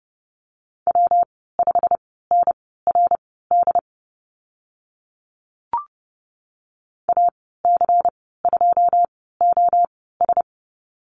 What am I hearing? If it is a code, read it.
Morse code, heard as a keyed tone, about 30 wpm, spaced as for 17 wpm, 700 Hz. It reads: W5NRD AC2OS
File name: Lesson 19g - Letter w - Call Signs 30wpm